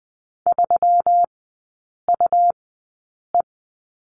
Morse code, heard as a keyed tone, about 20 wpm, spaced as for 8 wpm, 700 Hz. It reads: 3UE
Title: 178 Lesson 29e - Slash - Three Characters 20wpm